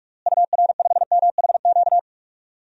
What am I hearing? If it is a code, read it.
Morse code, heard as a keyed tone, about 42 wpm, 700 Hz. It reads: UR5MHQ
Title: DXCC-100 Call Signs - 6 Characters - Repeated 3x - 45wpm